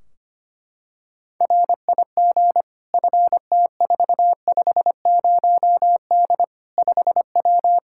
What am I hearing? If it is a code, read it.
Morse code, heard as a keyed tone, about 25 wpm, 700 Hz. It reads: RIG FT450D 5W
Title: Practice QSO - Exchange 2 - Info - 25wpm